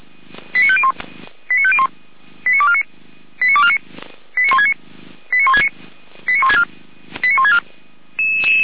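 Five-tone selective call (ZVEI), heard as quick runs of five tones, tones of about 70 ms. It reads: 89501 89501 89249 89249 89169 89169 89164 89164
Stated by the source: ZVEI five-tone selective call 89167, received by SDR at feuerwehr-orlamuende.org (fire brigade Orlamünde).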